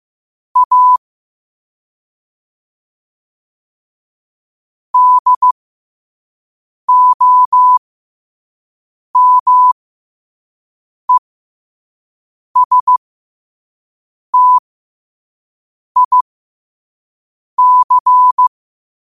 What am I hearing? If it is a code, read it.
Morse code, heard as a keyed tone, about 15 wpm, spaced as for 5 wpm, 1000 Hz. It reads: A DOMESTIC
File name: qotdfiles / 5wpm